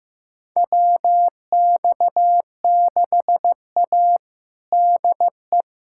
Morse code, heard as a keyed tone, about 15 wpm, 700 Hz. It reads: WX6A DE
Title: USA Callsign DE Callsign - 4 Characters - 15wpm